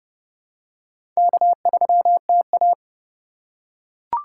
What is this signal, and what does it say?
Morse code, heard as a keyed tone, about 31 wpm, 700 Hz. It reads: K3TA
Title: USA Call Signs - 4 Characters - 2x Character Spacing - 30wpm